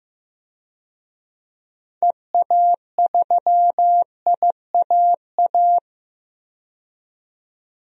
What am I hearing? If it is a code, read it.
Morse code, heard as a keyed tone, about 15 wpm, 700 Hz. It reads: EA3IAA
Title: DXCC-100 Call Signs - 6 Characters - 15wpm